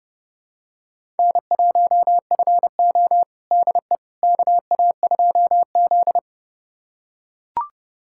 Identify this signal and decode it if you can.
Morse code, heard as a keyed tone, about 32 wpm, 700 Hz. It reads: N1FO DE KA2Z
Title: USA Callsign DE Callsign - 4 Characters - 30wpm